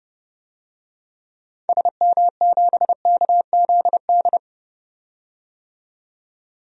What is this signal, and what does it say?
Morse code, heard as a keyed tone, about 32 wpm, 700 Hz. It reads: SM7KZD
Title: DXCC-100 Call Signs - 6 Characters - 30wpm